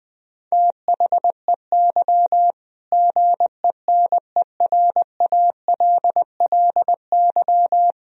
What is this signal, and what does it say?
Morse code, heard as a keyed tone, about 20 wpm, 700 Hz. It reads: THEY GENERALLY
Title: Sentences from Top 1000 Words - Review Part 4 - 20wpm